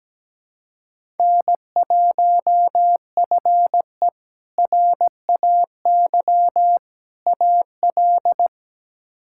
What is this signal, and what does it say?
Morse code, heard as a keyed tone, about 17 wpm, 700 Hz. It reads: N1FE RAY AL
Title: K1USN SST QSO Practice - 17wpm